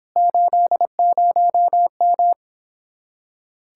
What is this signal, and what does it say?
Morse code, heard as a keyed tone, about 26 wpm, 700 Hz. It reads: 80M